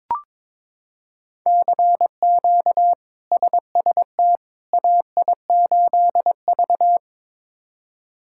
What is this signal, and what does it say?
Morse code, heard as a keyed tone, about 22 wpm, 700 Hz. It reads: CQ SST AI8V
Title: K1USN SST QSO Practice - 22wpm